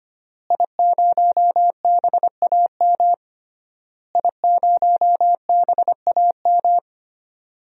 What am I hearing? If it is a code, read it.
Morse code, heard as a keyed tone, about 25 wpm, 700 Hz. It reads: I0BAM I0BAM
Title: DXCC-100 Call Signs - 5 Characters - Repeated 3x - 25wpm